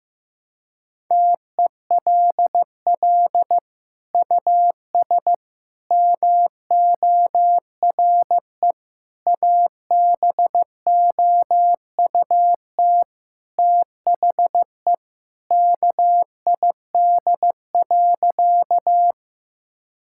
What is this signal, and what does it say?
Morse code, heard as a keyed tone, about 16 wpm, 700 Hz. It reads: TELL US MORE ABOUT THE KID.
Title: Sentences from Top 1600 Words 15wpm